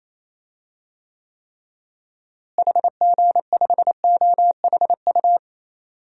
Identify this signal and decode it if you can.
Morse code, heard as a keyed tone, about 28 wpm, 700 Hz. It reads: HG5OHU
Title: DXCC-100 Call Signs - 6 Characters - Repeated 3x - 28wpm